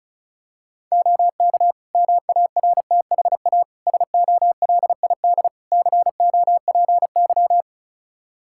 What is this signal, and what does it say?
Morse code, heard as a keyed tone, about 36 wpm, 700 Hz. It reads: OK MARTHA SOLID COPY
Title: Practice QSO - Exchange 3 - More Info - 35wpm